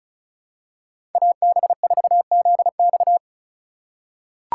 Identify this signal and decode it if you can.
Morse code, heard as a keyed tone, about 36 wpm, 700 Hz. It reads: AB4ZX
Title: USA Call Signs - 5 Characters - 3x Character Spacing - 35wpm